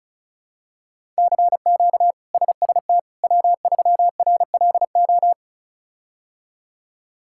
Morse code, heard as a keyed tone, about 34 wpm, 700 Hz. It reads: CQ SST W3RLO
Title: K1USN SST QSO Practice - 35wpm